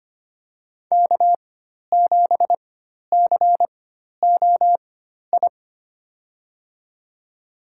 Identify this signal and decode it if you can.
Morse code, heard as a keyed tone, about 25 wpm, spaced as for 12 wpm, 700 Hz. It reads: K7COI
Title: USA Call Signs - 5 Characters - 4x Character Spacing - 25wpm